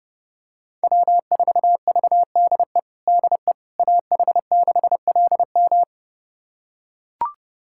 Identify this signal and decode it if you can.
Morse code, heard as a keyed tone, about 30 wpm, 700 Hz. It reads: W4VDE DE AH6LM
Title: USA Callsign DE Callsign - 5 Characters - 30wpm